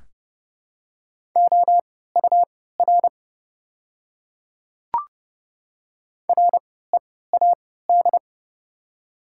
Morse code, heard as a keyed tone, about 30 wpm, spaced as for 17 wpm, 700 Hz. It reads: OUR READ
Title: 3x Character Spacing - Top 200 Words - 30wpm